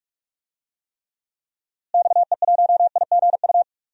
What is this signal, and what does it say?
Morse code, heard as a keyed tone, about 41 wpm, 700 Hz. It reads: XE1IGU
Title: DXCC-100 Call Signs - 6 Characters - Repeated 3x - 45wpm